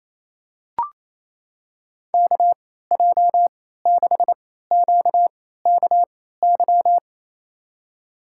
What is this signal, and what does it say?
Morse code, heard as a keyed tone, about 28 wpm, spaced as for 16 wpm, 700 Hz. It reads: KJ6QKY
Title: USA Call Signs - 6 Characters - 3x Character Spacing - 28wpm